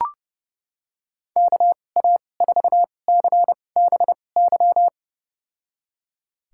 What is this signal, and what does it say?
Morse code, heard as a keyed tone, about 30 wpm, spaced as for 22 wpm, 700 Hz. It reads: KA4CBY
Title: USA Call Signs - 6 Characters - 2x Character Spacing - 30wpm